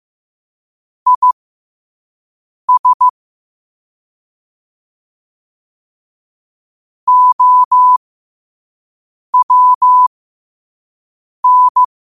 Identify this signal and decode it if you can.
Morse code, heard as a keyed tone, about 15 wpm, spaced as for 5 wpm, 1000 Hz. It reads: IS OWN
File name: qotdfiles / 5wpm